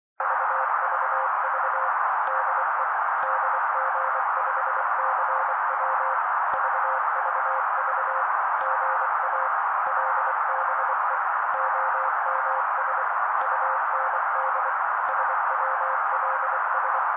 Morse code, heard as a keyed tone, about 24 wpm, 530 Hz. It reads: VVV DE DG5CW VVV GA LBE OMS UND SWLS
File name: daily / 23wpm noise